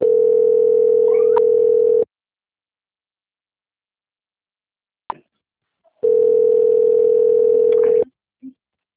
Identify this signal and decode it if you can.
Ringback tone (North America), 440 + 480 Hz.
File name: out- - - - - .38337